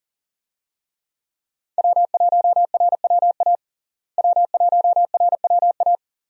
Morse code, heard as a keyed tone, about 40 wpm, 700 Hz. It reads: W1RWA W1RWA
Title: USA Call Signs - 5 Characters - Repeated 2x - 40wpm